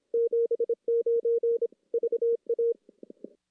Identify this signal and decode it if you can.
Morse code, heard as a keyed tone, about 26 wpm, 470 Hz. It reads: 79 VA
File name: contests / CW-SS-k5tr- -1600